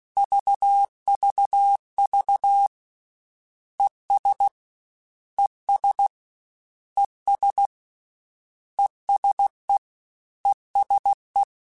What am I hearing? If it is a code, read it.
Morse code, heard as a keyed tone, about 16 wpm, 800 Hz. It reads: VVV ES ES ES ESE ESE
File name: Combinations / Combinations-mp3s / Combinations@15